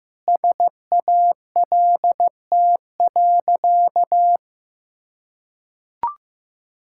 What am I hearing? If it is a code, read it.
Morse code, heard as a keyed tone, about 16 wpm, 700 Hz. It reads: SALT.
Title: Sentences from Top 1000 Words - Review Part 5 - 15wpm